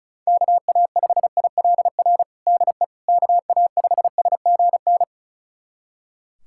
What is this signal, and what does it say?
Morse code, heard as a keyed tone, about 35 wpm, 700 Hz. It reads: KA5ILR DE KA5SGN
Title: USA Callsign DE Callsign - 6 Characters - 35wpm